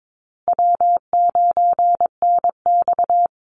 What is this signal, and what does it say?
Morse code, heard as a keyed tone, about 22 wpm, 700 Hz. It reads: W9NX
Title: Easy Call Signs Repeated 3x - 22wpm